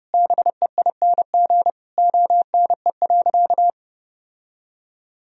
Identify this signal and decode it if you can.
Morse code, heard as a keyed tone, about 31 wpm, 700 Hz. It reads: BEING ONE.
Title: Sentences from Top 1000 Words - Review Part 3 - 30wpm